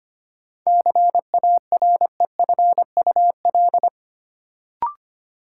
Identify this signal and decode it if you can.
Morse code, heard as a keyed tone, about 25 wpm, 700 Hz. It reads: CAREFUL
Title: Sentences from Top 1600 Words - New Word Drill Alt - 25wpm